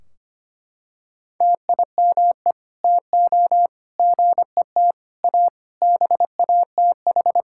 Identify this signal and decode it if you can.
Morse code, heard as a keyed tone, about 25 wpm, 700 Hz. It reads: TIME TO GET A BATH
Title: Sentences from Top 1600 Words 25wpm